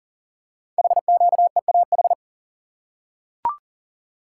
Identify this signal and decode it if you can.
Morse code, heard as a keyed tone, about 39 wpm, 690 Hz. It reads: HQEAH
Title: Code Groups - 5 Letters - 40wpm